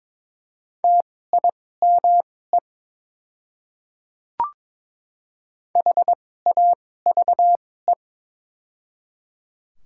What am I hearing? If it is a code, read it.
Morse code, heard as a keyed tone, about 22 wpm, spaced as for 16 wpm, 700 Hz. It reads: TIME HAVE
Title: 2x Character Spacing - Top 100 Words - 22wpm